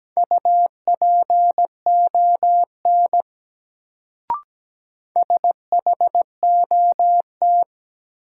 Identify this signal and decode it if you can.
Morse code, heard as a keyed tone, about 17 wpm, 700 Hz. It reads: UPON SHOT
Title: Sentences from Top 1100 Words - New Word Drill Alt - 17wpm